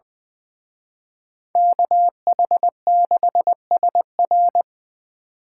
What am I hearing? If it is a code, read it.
Morse code, heard as a keyed tone, about 24 wpm, 700 Hz. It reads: KH6SR
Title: Speed Racing - USA 2x2 Call Signs 20wpm